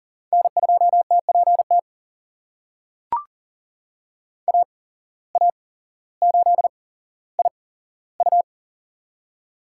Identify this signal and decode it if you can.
Morse code, heard as a keyed tone, about 40 wpm, spaced as for 17 wpm, 700 Hz. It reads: N2TPT AA8IU
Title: USA Call Signs - 5 Characters - 8x Character Spacing - 40wpm